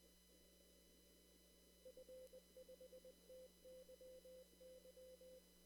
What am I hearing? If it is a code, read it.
Morse code, heard as a keyed tone, about 20 wpm, 520 Hz. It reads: F5TYY
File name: CW / kt5j wpx cw-op-k5pi